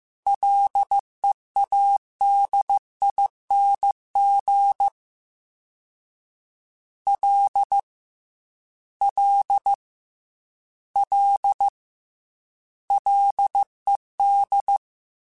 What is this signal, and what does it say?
Morse code, heard as a keyed tone, about 15 wpm, 800 Hz. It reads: LEADING L L L LED
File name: F and L / F and L mp3 / F and L Prefixes 3x2 Twice @13